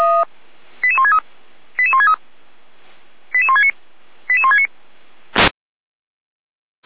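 Five-tone selective call (ZVEI), heard as quick runs of five tones, tones of about 70 ms. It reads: 88163 88163 88169 88169
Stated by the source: ZVEI five-tone call 88365 received on SDR.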